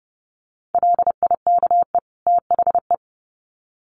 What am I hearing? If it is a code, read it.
Morse code, heard as a keyed tone, about 30 wpm, 700 Hz. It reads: LIKE THE
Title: Sets of 2 Words 30wpm